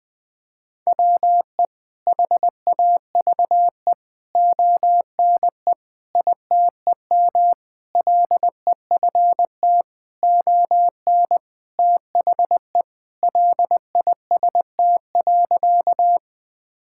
Morse code, heard as a keyed tone, about 21 wpm, 700 Hz. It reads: WE HAVE ONE ITEM LEFT ON THE LIST.
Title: Sentences from Top 1000 Words - Review Part 6 - 20wpm